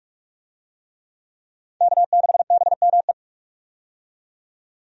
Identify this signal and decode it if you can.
Morse code, heard as a keyed tone, about 44 wpm, 700 Hz. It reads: K6BME
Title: USA Call Signs - 5 Characters - 45wpm